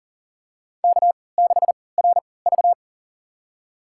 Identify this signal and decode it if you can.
Morse code, heard as a keyed tone, about 40 wpm, spaced as for 23 wpm, 700 Hz. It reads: K6RV
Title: USA Call Signs - 4 Characters - 3x Character Spacing - 40wpm